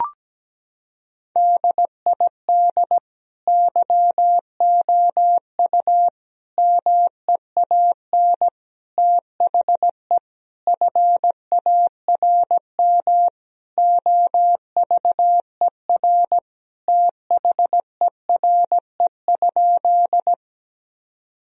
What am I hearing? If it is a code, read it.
Morse code, heard as a keyed tone, about 17 wpm, 700 Hz. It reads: DID YOU MEAN THE FARM OVER THERE?
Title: Sentences from Top 1000 Words - Review Part 2 - 17wpm